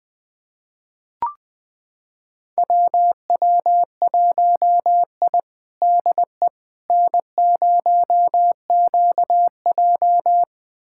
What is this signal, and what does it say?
Morse code, heard as a keyed tone, about 20 wpm, 700 Hz. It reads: WW1I DE N0QJ
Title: USA Callsign DE Callsign - 4 Characters - 20wpm